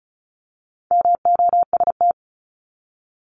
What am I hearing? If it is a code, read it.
Morse code, heard as a keyed tone, about 34 wpm, 700 Hz. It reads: MOST
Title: Top 200 Words 35wpm